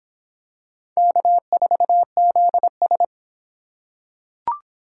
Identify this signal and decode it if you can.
Morse code, heard as a keyed tone, about 25 wpm, 700 Hz. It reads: K4ZS